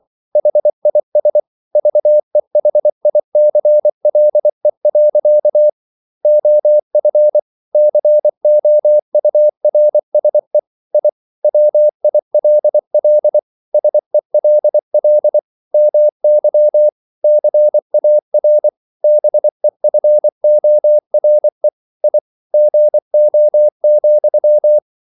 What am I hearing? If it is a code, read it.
Morse code, heard as a keyed tone, about 24 wpm, 600 Hz. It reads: HIS VEHICLE. OF COURSE I WILL SELL MY CAR BEFORE I GO,